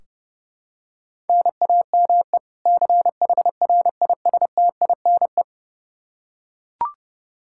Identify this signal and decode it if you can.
Morse code, heard as a keyed tone, about 30 wpm, 700 Hz. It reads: NAME CHRISTINE
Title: Practice QSO - Exchange 1 - Beginning - 30wpm